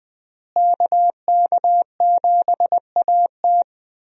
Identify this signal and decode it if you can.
Morse code, heard as a keyed tone, about 20 wpm, 700 Hz. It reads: KK7AT
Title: USA Call Signs - 5 Characters - Repeated 3x - 20wpm